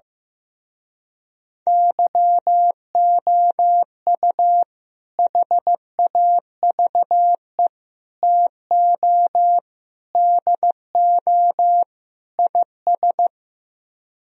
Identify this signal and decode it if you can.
Morse code, heard as a keyed tone, about 15 wpm, 700 Hz. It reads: YOU HAVE TO DO IS
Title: Speed-Racing - Sets of 5 Words 15wpm